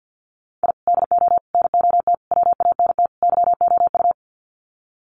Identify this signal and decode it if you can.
Morse code, heard as a keyed tone, about 50 wpm, 700 Hz. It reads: I DO NOT WANT YOU
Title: Sets of 5 Words 50wpm